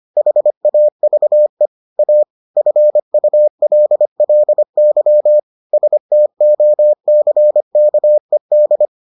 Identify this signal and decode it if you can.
Morse code, heard as a keyed tone, about 25 wpm, 600 Hz. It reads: HAVE A FULLY STOCKED